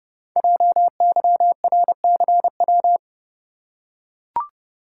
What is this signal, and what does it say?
Morse code, heard as a keyed tone, about 30 wpm, 700 Hz. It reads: JYRCW
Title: Code Groups - 5 Letter:Numbers - 30wpm